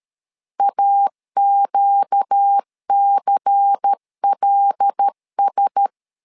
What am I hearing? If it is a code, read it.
Morse code, heard as a keyed tone, about 13 wpm, 800 Hz. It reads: AQCLS
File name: LEZIONI / 5 LBVC / 5 LBVC 12wpm 1955